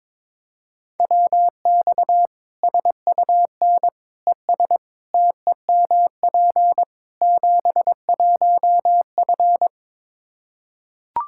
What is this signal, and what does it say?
Morse code, heard as a keyed tone, about 22 wpm, 700 Hz. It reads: WX SUN ES TEMP 71F
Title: Practice QSO - Exchange 2 - Info - 22wpm